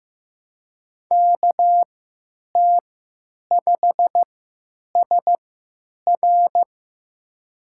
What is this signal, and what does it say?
Morse code, heard as a keyed tone, about 15 wpm, spaced as for 9 wpm, 700 Hz. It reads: KT5SR
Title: USA Call Signs - 3x Character Spacing - 15wpm